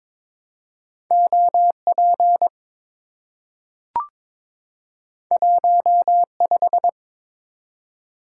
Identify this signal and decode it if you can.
Morse code, heard as a keyed tone, about 22 wpm, 700 Hz. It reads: OP 15
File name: practice / 22wpm / Two Letter-Number Combinations - 22wpm